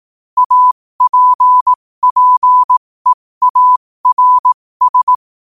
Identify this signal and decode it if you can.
Morse code, heard as a keyed tone, about 18 wpm, spaced as for 15 wpm, 1000 Hz. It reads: APPEARS
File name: qotdfiles / 15wpm